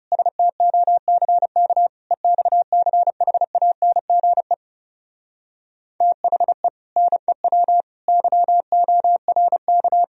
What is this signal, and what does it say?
Morse code, heard as a keyed tone, about 34 wpm, 700 Hz. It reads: STOCK EXCHANGE THE NEW YORK
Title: Speed-Racing - Sets of 5 Words 30wpm